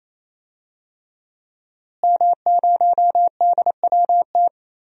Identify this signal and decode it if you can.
Morse code, heard as a keyed tone, about 28 wpm, 700 Hz. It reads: M0DWT
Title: DXCC-100 Call Signs - 5 Characters - Repeated 3x - 28wpm